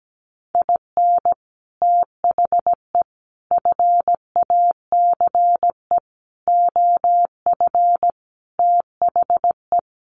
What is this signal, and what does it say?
Morse code, heard as a keyed tone, about 17 wpm, 700 Hz. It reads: IN THE FACE OF THE
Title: Sets of 5 Words 17wpm